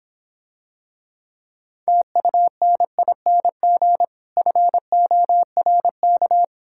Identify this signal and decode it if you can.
Morse code, heard as a keyed tone, about 26 wpm, 700 Hz. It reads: TUNING FORK